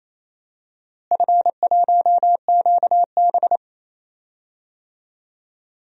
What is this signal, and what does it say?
Morse code, heard as a keyed tone, about 27 wpm, 700 Hz. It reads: F1QB
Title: DXCC-100 Call Signs - 4 Characters - Repeated 2x - 28wpm